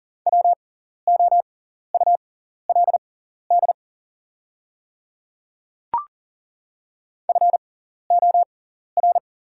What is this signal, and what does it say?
Morse code, heard as a keyed tone, about 40 wpm, spaced as for 14 wpm, 700 Hz. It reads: WOULD FOR
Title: 6x Character Spacing - Top 100 Words - 40wpm